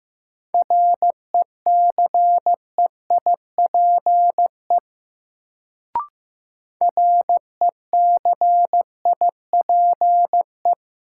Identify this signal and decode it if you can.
Morse code, heard as a keyed tone, about 15 wpm, 700 Hz. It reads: RECEIPE RECIPE
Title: Sentences from Top 1200 Words - New Word Drill Alt - 15wpm